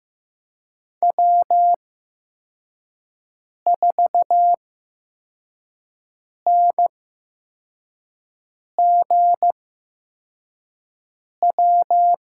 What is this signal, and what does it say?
Morse code, heard as a keyed tone, about 15 wpm, spaced as for 4 wpm, 700 Hz. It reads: W4NGW
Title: USA Call Signs - 5 Characters - 8x Character Spacing - 15wpm